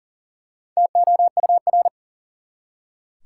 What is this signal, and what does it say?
Morse code, heard as a keyed tone, about 40 wpm, 700 Hz. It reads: TOUR
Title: Four Letter Words - 40wpm